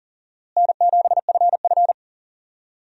Morse code, heard as a keyed tone, about 39 wpm, 700 Hz. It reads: N7FF
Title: USA Call Signs - 4 Characters - Repeated 2x - 40wpm